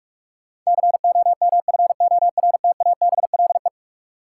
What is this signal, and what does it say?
Morse code, heard as a keyed tone, about 46 wpm, 700 Hz. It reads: COMFORTABLE
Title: Sentences from Top 1000 Words - Review Part 7 - New Word Drill - 45wpm